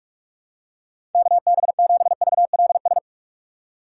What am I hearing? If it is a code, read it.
Morse code, heard as a keyed tone, about 46 wpm, 700 Hz. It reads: KB7VLS
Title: USA Call Signs - 6 Characters - 4x Character Spacing - 45wpm